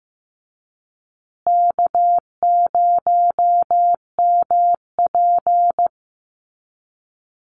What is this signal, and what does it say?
Morse code, heard as a keyed tone, about 14 wpm, 700 Hz. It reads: K0MP
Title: Call Signs Repeated 3x - 15wpm